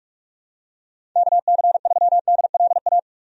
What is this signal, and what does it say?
Morse code, heard as a keyed tone, about 46 wpm, 700 Hz. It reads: KC3DLA
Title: USA Call Signs - 6 Characters - 45wpm